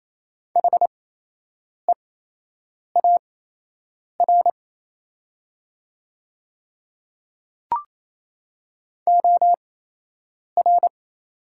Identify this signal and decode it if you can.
Morse code, heard as a keyed tone, about 28 wpm, spaced as for 8 wpm, 700 Hz. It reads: HEAR OR
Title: 8x Character Spacing - Top 300 Words - 28wpm